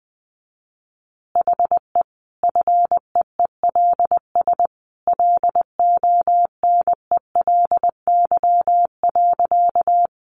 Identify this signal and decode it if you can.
Morse code, heard as a keyed tone, about 20 wpm, 700 Hz. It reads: HE FEELS LONELY.
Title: Easy Phrases 20wpm